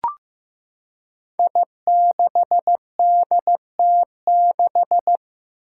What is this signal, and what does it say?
Morse code, heard as a keyed tone, about 15 wpm, 700 Hz. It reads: I6DT6
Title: Five Challenging Letter-Number Combinations 15wpm